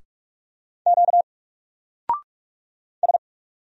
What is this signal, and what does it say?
Morse code, heard as a keyed tone, about 44 wpm, 700 Hz. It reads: Q S
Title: Single Letter-Number 45wpm